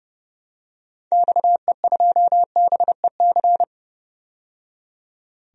Morse code, heard as a keyed tone, about 31 wpm, 700 Hz. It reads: XE2BEC
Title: DXCC-100 Call Signs - 6 Characters - 30wpm